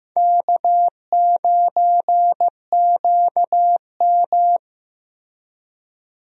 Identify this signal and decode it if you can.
Morse code, heard as a keyed tone, about 15 wpm, 700 Hz. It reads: K9QM
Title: USA Call Signs - 4 Characters - Repeated 2x - 15wpm